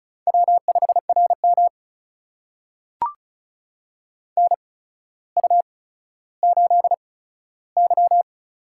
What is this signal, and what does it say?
Morse code, heard as a keyed tone, about 36 wpm, spaced as for 15 wpm, 700 Hz. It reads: W5RM NU8Y
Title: USA Call Signs - 4 Characters - 8x Character Spacing - 35wpm